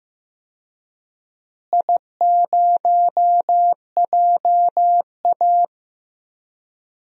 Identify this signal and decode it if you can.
Morse code, heard as a keyed tone, about 15 wpm, 700 Hz. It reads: I0JA
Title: DXCC-100 Call Signs - 4 Characters - Repeated 2x - 15wpm